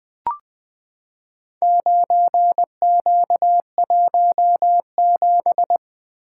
Morse code, heard as a keyed tone, about 20 wpm, 700 Hz. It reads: 9Q17
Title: Four Letter-Number Combinations - 20wpm